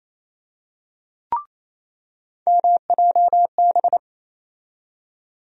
Morse code, heard as a keyed tone, about 28 wpm, 700 Hz. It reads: MJB
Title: USA Call Sign Postfixes - 3 Letters 28wpm